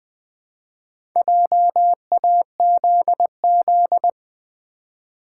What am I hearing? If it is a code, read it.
Morse code, heard as a keyed tone, about 20 wpm, 700 Hz. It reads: JAZZ
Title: Build A Word - 2 Letters At A Time - 4-Letter Words - 20wpm